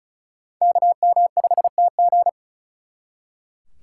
Morse code, heard as a keyed tone, about 44 wpm, 700 Hz. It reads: KM5TG